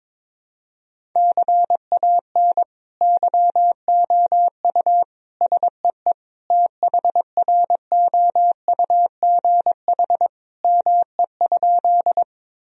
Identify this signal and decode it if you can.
Morse code, heard as a keyed tone, about 22 wpm, 700 Hz. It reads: CAN YOU SEE THROUGH ME?
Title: Sentences from Top 1000 Words - Review Part 1 - 22wpm